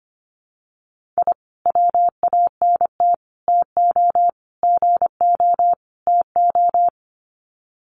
Morse code, heard as a keyed tone, about 25 wpm, 700 Hz. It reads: I WANT TO GO TO
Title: Sets of 5 Words Encore 25wpm